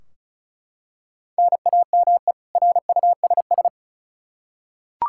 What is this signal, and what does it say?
Morse code, heard as a keyed tone, about 34 wpm, 700 Hz. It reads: NAME RUSS
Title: Practice QSO - Exchange 1 - Beginning - 35wpm